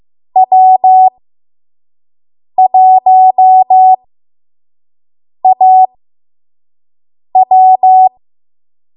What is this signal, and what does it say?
Morse code, heard as a keyed tone, about 15 wpm, spaced as for 5 wpm, 750 Hz. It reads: W1AW
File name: Archive / 5 WPM / 05WPM